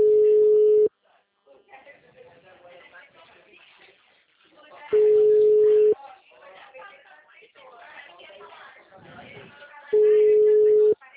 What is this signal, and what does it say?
Ringback tone (Europe, 425 Hz), with speech.
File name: out- -2001- - - .89534